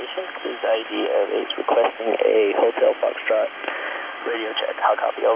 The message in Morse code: none

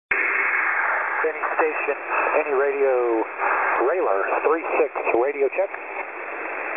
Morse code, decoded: none

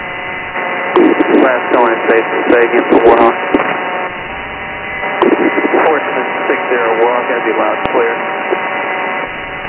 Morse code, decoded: none

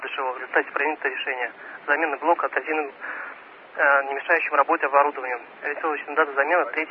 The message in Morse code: none

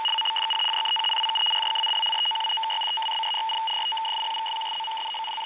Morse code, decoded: none

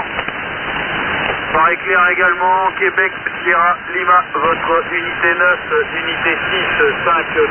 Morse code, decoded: none

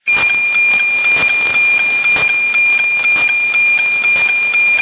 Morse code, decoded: none

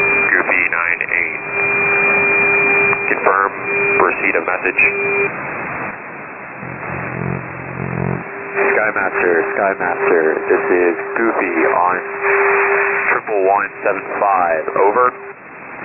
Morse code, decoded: none

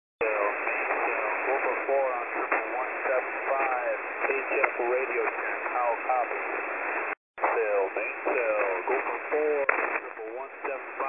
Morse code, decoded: none

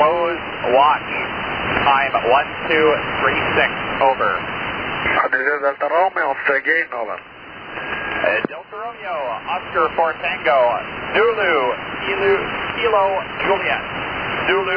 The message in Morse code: none

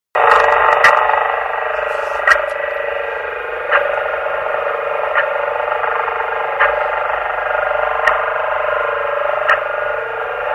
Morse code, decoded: none

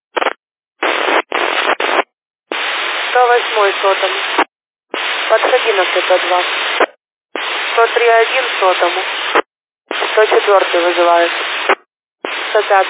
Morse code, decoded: none

